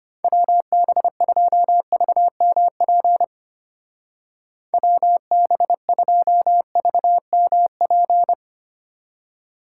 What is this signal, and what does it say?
Morse code, heard as a keyed tone, about 30 wpm, 700 Hz. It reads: WB2VMP WB2VMP
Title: Speed Racing - USA Call Signs 25wpm